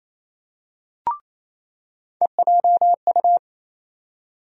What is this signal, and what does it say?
Morse code, heard as a keyed tone, about 28 wpm, 700 Hz. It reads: EJU
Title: USA Call Sign Postfixes - 3 Letters 28wpm